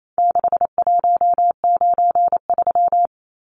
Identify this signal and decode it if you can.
Morse code, heard as a keyed tone, about 28 wpm, 700 Hz. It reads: 6193
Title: Numbers - 4-Digit 28wpm